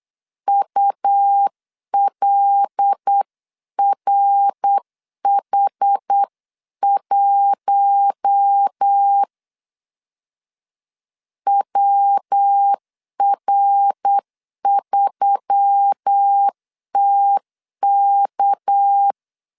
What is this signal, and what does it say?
Morse code, heard as a keyed tone, about 8 wpm, 800 Hz. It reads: ULRH1 WR3TK
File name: CW / DET / 1 6 1500C S C 8wpm 5219